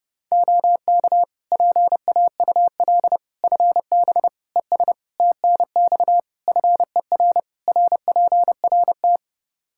Morse code, heard as a keyed tone, about 30 wpm, 700 Hz. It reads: OK PAUL FB ES TNX FER RPRT
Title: Practice QSO - Exchange 2 - Info - 30wpm